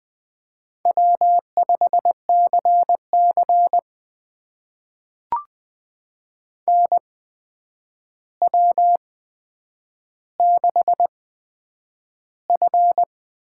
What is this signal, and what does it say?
Morse code, heard as a keyed tone, about 20 wpm, spaced as for 9 wpm, 700 Hz. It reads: W5CC NW6F
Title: USA Call Signs - 4 Characters - 8x Character Spacing - 20wpm